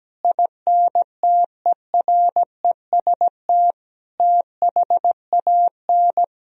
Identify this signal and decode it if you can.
Morse code, heard as a keyed tone, about 17 wpm, 700 Hz. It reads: INTEREST THAN
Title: Sentences from Top 400 Words and Abbreviations 17wpm